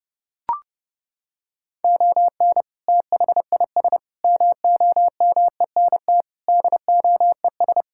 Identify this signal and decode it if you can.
Morse code, heard as a keyed tone, about 30 wpm, 700 Hz. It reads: ON THIS MOMENT DOES
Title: Sentences from Top 1600 Words 30wpm